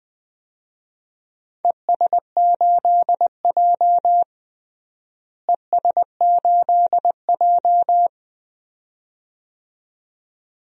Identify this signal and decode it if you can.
Morse code, heard as a keyed tone, about 21 wpm, 700 Hz. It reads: ES8J ES8J
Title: DXCC-100 Call Signs - 4 Characters - Repeated 2x - 20wpm